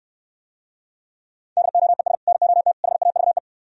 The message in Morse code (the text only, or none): DOES NOT HAVE